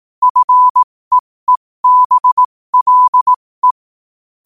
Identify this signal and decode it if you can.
Morse code, heard as a keyed tone, about 18 wpm, spaced as for 15 wpm, 1000 Hz. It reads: FEEBLE
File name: qotdfiles / 15wpm